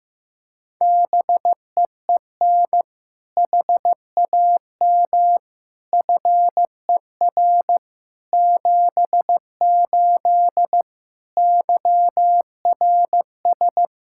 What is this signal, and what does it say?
Morse code, heard as a keyed tone, about 15 wpm, 700 Hz. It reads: BEEN HAM FER 78 YRS
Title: K1USN SST QSO Practice 15wpm